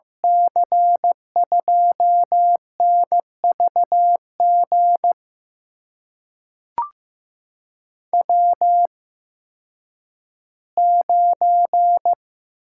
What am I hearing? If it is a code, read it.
Morse code, heard as a keyed tone, about 15 wpm, spaced as for 10 wpm, 700 Hz. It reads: C2NVG W9
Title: USA Call Signs - 8x Character Spacing - 15wpm